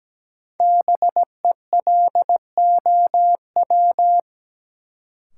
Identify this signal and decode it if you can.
Morse code, heard as a keyed tone, about 17 wpm, 700 Hz. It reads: BELOW
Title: Sentences from Top 1000 Words - Review Part 5 - New Word Drill - 17wpm